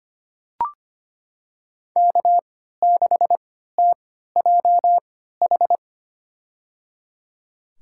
Morse code, heard as a keyed tone, about 25 wpm, spaced as for 14 wpm, 700 Hz. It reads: K6TJH
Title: USA Call Signs - 5 Characters - 3x Character Spacing - 25wpm